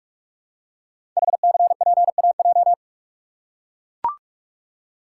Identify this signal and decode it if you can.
Morse code, heard as a keyed tone, about 44 wpm, 700 Hz. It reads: HCPAJ
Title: Code Groups - 5 Letters - 45wpm